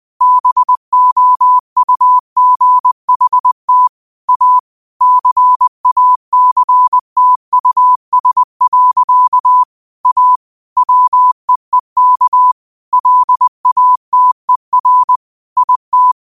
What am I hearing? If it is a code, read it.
Morse code, heard as a keyed tone, about 20 wpm, 1000 Hz. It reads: BOUGHT A CACTUS. A WEEK LATER IT
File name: qotdfiles / 20wpm